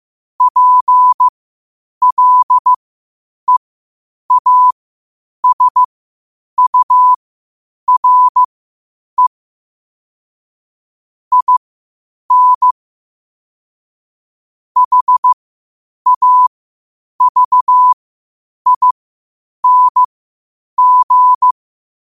Morse code, heard as a keyed tone, about 15 wpm, spaced as for 8 wpm, 1000 Hz. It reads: PLEASURE IN HAVING
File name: qotdfiles / 8wpm